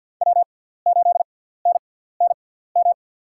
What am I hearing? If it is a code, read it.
Morse code, heard as a keyed tone, about 47 wpm, spaced as for 17 wpm, 700 Hz. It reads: W8NNM
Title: USA Call Signs - 6x Character Spacing - 50wpm